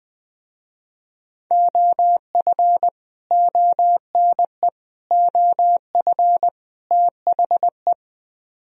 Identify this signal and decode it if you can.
Morse code, heard as a keyed tone, about 21 wpm, 700 Hz. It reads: OF ONE OF THE